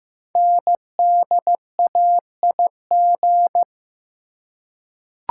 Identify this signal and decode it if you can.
Morse code, heard as a keyed tone, about 15 wpm, 700 Hz. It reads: NDAIG
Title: Code Groups - 5 Letters - 15wpm